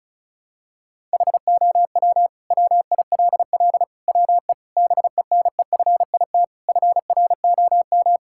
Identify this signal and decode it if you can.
Morse code, heard as a keyed tone, about 34 wpm, 700 Hz. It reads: HOW WILL WE BENEFIT FROM
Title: Sentences from Top 1000 Words - Review Part 6 - 35wpm